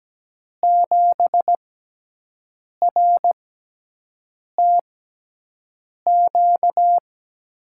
Morse code, heard as a keyed tone, about 17 wpm, spaced as for 6 wpm, 700 Hz. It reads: 7RTQ
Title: USA Call Signs - 6x Character Spacing - 17wpm